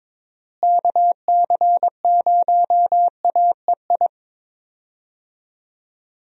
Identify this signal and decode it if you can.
Morse code, heard as a keyed tone, about 22 wpm, 700 Hz. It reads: KC0AEI
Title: USA Call Signs - 6 Characters - Repeated 2x - 22wpm